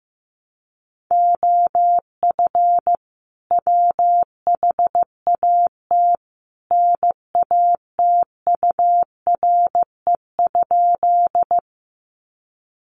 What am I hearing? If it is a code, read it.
Morse code, heard as a keyed tone, about 15 wpm, 700 Hz. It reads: OF WHAT NATURE?
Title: Sentences from Top 600 Words 15wpm